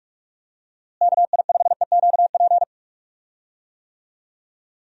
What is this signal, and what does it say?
Morse code, heard as a keyed tone, about 46 wpm, 700 Hz. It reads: KI5EQP
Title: USA Call Signs - 6 Characters - Repeated 3x - 45wpm